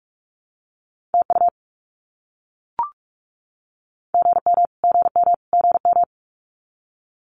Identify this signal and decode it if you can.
Morse code, heard as a keyed tone, about 44 wpm, 700 Hz. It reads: TU GM GM GM
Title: Abbreviations Repeated 3x - 45wpm